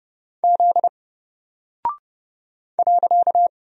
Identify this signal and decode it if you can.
Morse code, heard as a keyed tone, about 31 wpm, 700 Hz. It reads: Z .
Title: Single Character (Letter:Number:Punctuation:Prosign) Alt 30wpm